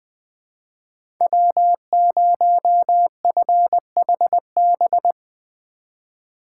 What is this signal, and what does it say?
Morse code, heard as a keyed tone, about 20 wpm, 700 Hz. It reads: W0FHB
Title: USA Call Signs - 5 Characters - 2x Character Spacing - 20wpm